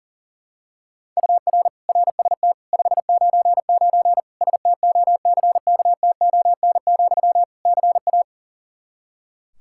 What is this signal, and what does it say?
Morse code, heard as a keyed tone, about 40 wpm, 700 Hz. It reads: UR RST 599 STOCKTON, CA
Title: K1USN SST QSO Practice 40wpm